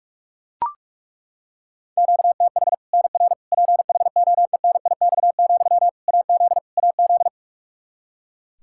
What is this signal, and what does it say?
Morse code, heard as a keyed tone, about 41 wpm, 700 Hz. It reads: QTH NR PHOENIX, AZ AZ
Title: K1USN SST QSO Practice 45wpm